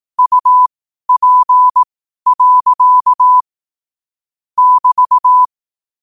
Morse code, heard as a keyed tone, about 18 wpm, spaced as for 12 wpm, 1000 Hz. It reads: UP. <BT>
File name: 12wpm